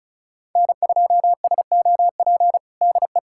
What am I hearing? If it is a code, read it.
Morse code, heard as a keyed tone, about 36 wpm, 690 Hz. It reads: N2SOP DE
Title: USA Callsign DE Callsign - 5 Characters - 35wpm